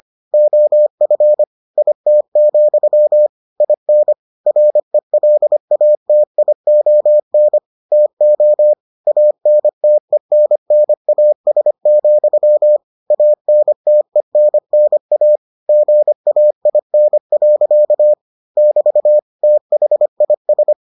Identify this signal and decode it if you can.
Morse code, heard as a keyed tone, about 25 wpm, 600 Hz. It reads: OF IT, IN RELATION TO ANTENNAS, ANTENNA GAIN. <BT> THIS